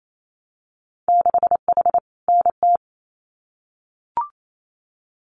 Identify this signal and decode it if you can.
Morse code, heard as a keyed tone, about 28 wpm, 700 Hz. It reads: 6H NT
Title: Winter Field Day Exchanges Encore 28wpm